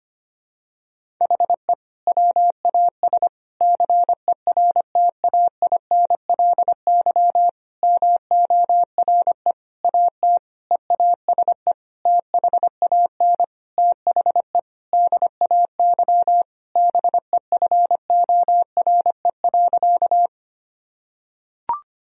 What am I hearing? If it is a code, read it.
Morse code, heard as a keyed tone, about 25 wpm, 700 Hz. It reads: HE WAS CERTAINLY MORE AT EASE THAN THE DAY BEFORE.
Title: Sentences from Top 1000 Words - Review Part 1 - 25wpm